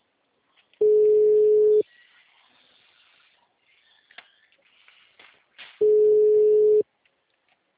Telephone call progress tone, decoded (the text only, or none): ringback tone